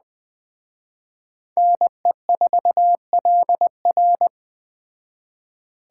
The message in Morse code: NE4LR

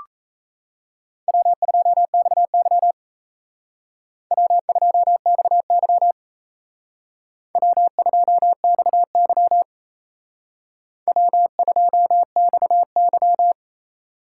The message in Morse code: W2XY W2XY W2XY W2XY

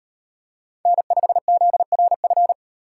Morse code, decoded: N5ZRF

